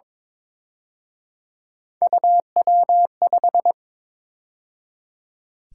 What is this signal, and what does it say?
Morse code, heard as a keyed tone, about 26 wpm, 700 Hz. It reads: UW5
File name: Speed-Racing - Three Letter-Number Combinations - 22wpm